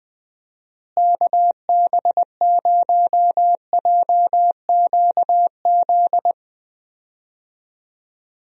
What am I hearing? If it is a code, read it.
Morse code, heard as a keyed tone, about 20 wpm, 700 Hz. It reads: KB0JQZ
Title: USA Call Signs - 6 Characters - Repeated 3x - 20wpm